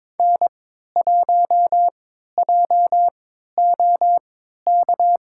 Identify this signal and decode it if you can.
Morse code, heard as a keyed tone, about 22 wpm, spaced as for 13 wpm, 700 Hz. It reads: N1JOK
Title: USA Call Signs - 5 Characters - 3x Character Spacing - 22wpm